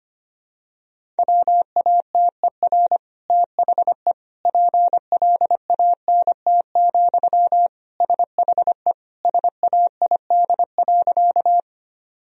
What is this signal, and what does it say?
Morse code, heard as a keyed tone, about 25 wpm, 700 Hz. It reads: WATER THE PLANT, SHE SAID.
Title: Sentences from Top 1000 Words - Review Part 1 - 25wpm